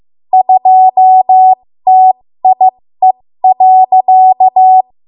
Morse code, heard as a keyed tone, about 15 wpm, spaced as for 13 wpm, 750 Hz. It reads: 2TIE.
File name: Morse / 13WPM